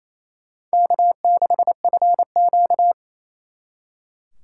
Morse code, heard as a keyed tone, about 36 wpm, 700 Hz. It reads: K6FQ